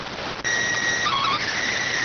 SSTV (slow-scan television), PD 290.